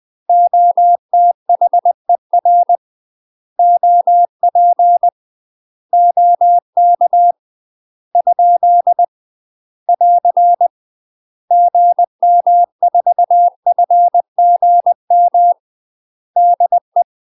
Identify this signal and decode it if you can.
Morse code, heard as a keyed tone, about 20 wpm, 700 Hz. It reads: OTHER OP OK ? <AR> GM4FGM DE
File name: UK MASTER QSOs TRAINING / 20WPM / UK MASTER QSOs 0050